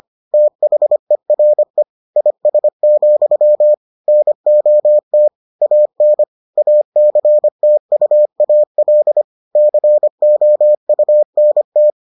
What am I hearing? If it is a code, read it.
Morse code, heard as a keyed tone, about 25 wpm, 600 Hz. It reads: THERE IS, NOT AN ACTUAL COUNT